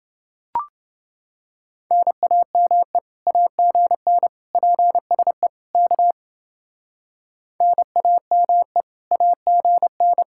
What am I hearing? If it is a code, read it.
Morse code, heard as a keyed tone, about 29 wpm, 700 Hz. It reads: NAME AGN PSE K NAME AGN